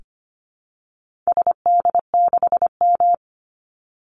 Morse code, heard as a keyed tone, about 25 wpm, 700 Hz. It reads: SD6M
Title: Call Signs - 25wpm